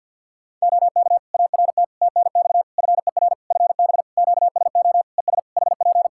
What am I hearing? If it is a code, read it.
Morse code, heard as a keyed tone, about 53 wpm, 700 Hz. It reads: OK ART TNX FER FB QSO ES HP
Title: Practice QSO - Exchange 4 - Ending - 50wpm